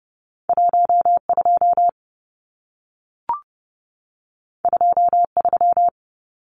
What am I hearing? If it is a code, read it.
Morse code, heard as a keyed tone, about 30 wpm, 700 Hz. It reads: 12 23
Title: Numbers - 2-Digit 30wpm